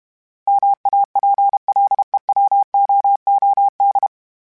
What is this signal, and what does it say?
Morse code, heard as a keyed tone, about 32 wpm, 800 Hz. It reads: MAPLEWOOD
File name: fromSMs / FromSMs mp3s / Cities@30